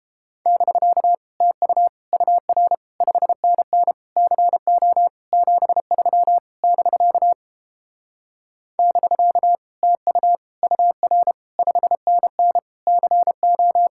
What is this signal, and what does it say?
Morse code, heard as a keyed tone, about 31 wpm, 700 Hz. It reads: <BK> TU UR 5NN CO 73 <BK> <BK> TU UR 5NN CO